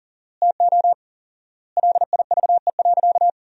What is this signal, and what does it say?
Morse code, heard as a keyed tone, about 40 wpm, 700 Hz. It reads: TO LIVE.
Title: 4x Word-Spacing - Sentences from Top 300 Words - 40wpm